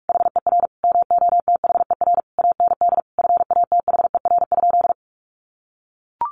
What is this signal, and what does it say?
Morse code, heard as a keyed tone, about 44 wpm, 700 Hz. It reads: HER MOTHER AND FATHER?
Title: Sentences from Top 1200 Words 45wpm